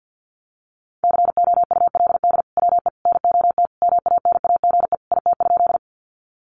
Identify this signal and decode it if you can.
Morse code, heard as a keyed tone, about 50 wpm, 700 Hz. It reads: COULD WE NOT MANAGE IT?
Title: Sentences from Top 1400 Words 50wpm